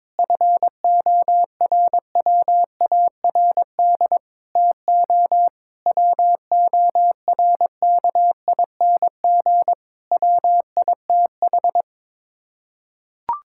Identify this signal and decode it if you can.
Morse code, heard as a keyed tone, about 22 wpm, 700 Hz. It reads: FORWARD TO WORKING WITH
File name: practice / 22wpm / Sets of 6 Words - 22wpm